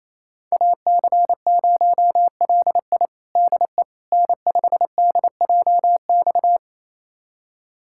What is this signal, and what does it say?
Morse code, heard as a keyed tone, about 28 wpm, 700 Hz. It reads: AC0LI DE N5DJX
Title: USA Callsign DE Callsign 28wpm